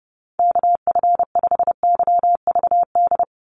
Morse code, heard as a keyed tone, about 30 wpm, 700 Hz. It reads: KF5YVD
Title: USA Call Signs 30wpm